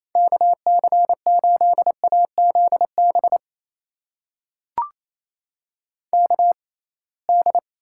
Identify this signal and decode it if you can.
Morse code, heard as a keyed tone, about 28 wpm, spaced as for 21 wpm, 700 Hz. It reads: KC8AZB KD
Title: USA Call Signs - 6 Characters - 6x Character Spacing - 28wpm